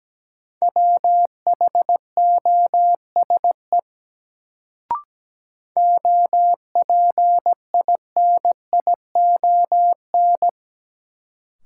Morse code, heard as a keyed tone, about 17 wpm, 700 Hz. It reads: WHOSE OPINION
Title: Sentences from Top 1200 Words - New Word Drill Alt - 17wpm